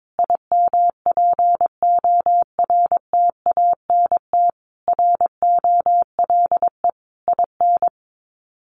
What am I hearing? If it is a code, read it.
Morse code, heard as a keyed tone, about 22 wpm, 700 Hz. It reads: IMPORTANT ROLE IN
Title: Sets of 5 Words Encore 22wpm